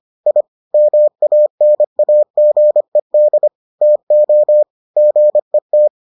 Morse code, heard as a keyed tone, about 25 wpm, 600 Hz. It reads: I MANAGED TO GET